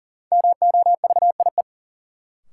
Morse code, heard as a keyed tone, about 40 wpm, 700 Hz. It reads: MOVIE
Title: Sentences from Top 1000 Words - Review Part 5 - New Word Drill - 40wpm